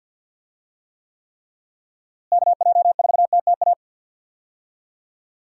Morse code, heard as a keyed tone, about 51 wpm, 700 Hz. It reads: KJ4TTA
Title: USA Call Signs - 6 Characters - Repeated 3x - 50wpm